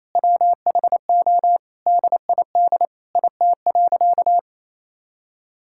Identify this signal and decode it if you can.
Morse code, heard as a keyed tone, about 28 wpm, 700 Hz. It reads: WHO DID IT.
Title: Sentences from Top 1000 Words - Review Part 6 - 28wpm